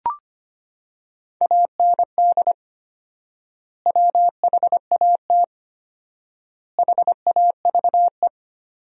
Morse code, heard as a keyed tone, about 25 wpm, 700 Hz. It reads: AND WHAT HAVE
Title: 4x Word-Spacing - Sentences from Top 500 Words - 25wpm